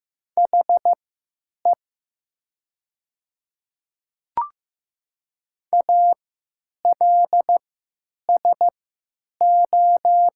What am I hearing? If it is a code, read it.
Morse code, heard as a keyed tone, about 15 wpm, spaced as for 8 wpm, 700 Hz. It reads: HE ALSO
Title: 3x Character Spacing - Top 200 Words - 15wpm